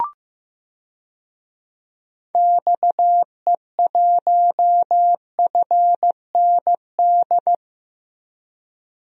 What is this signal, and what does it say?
Morse code, heard as a keyed tone, about 15 wpm, 700 Hz. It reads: XE1FND
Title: DXCC-100 Call Signs - 6 Characters - Repeated 3x - 15wpm